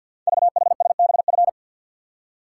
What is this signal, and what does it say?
Morse code, heard as a keyed tone, about 51 wpm, 700 Hz. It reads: VHSBF
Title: Code Groups - 5 Letters - 50wpm